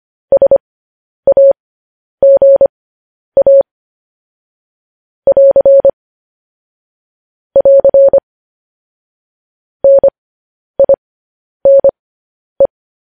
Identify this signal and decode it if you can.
Morse code, heard as a keyed tone, about 25 wpm, spaced as for 10 wpm, 550 Hz. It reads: SAGA <AR> <AR> NINE
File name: podcast / News headlines-1x-25x10